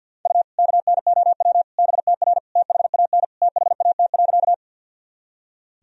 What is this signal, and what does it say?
Morse code, heard as a keyed tone, about 47 wpm, 700 Hz. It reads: U KNOW BTR THAN THAT.
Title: Sentences from Top 400 Words and Abbreviations 50wpm